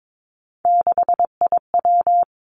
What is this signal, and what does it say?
Morse code, heard as a keyed tone, about 22 wpm, 700 Hz. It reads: 6IW
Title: Three Challenging Letter-Number Combinations 22wpm